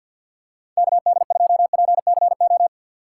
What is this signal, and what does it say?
Morse code, heard as a keyed tone, about 53 wpm, 700 Hz. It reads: KD2PCO
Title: USA Call Signs - 6x Character Spacing - 50wpm